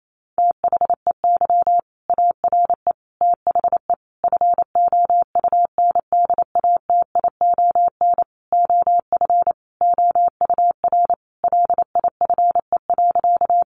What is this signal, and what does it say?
Morse code, heard as a keyed tone, about 28 wpm, 700 Hz. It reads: THEY ARE THE FOUNDATION OF OUR LIFE.
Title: Sentences from Top 1200 Words 28wpm